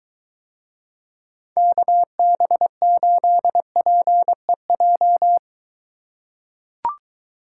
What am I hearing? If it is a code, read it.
Morse code, heard as a keyed tone, about 23 wpm, 700 Hz. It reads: KB8PEJ